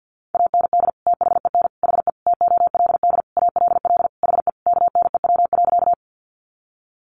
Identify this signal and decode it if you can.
Morse code, heard as a keyed tone, about 48 wpm, 700 Hz. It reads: AND THEN HE TOLD ALL HE KNEW.
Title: Sentences from Top 400 Words 50wpm